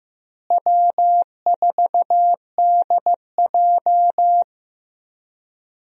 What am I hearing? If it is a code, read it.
Morse code, heard as a keyed tone, about 14 wpm, 700 Hz. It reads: W4DJ
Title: USA Call Signs - 4 Characters - Repeated 3x - 15wpm